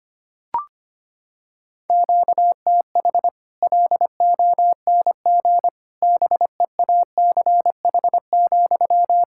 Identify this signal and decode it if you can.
Morse code, heard as a keyed tone, about 25 wpm, 700 Hz. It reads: QTH LONG BEACH,
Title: Practice QSO - Exchange 1 - Beginning - 25wpm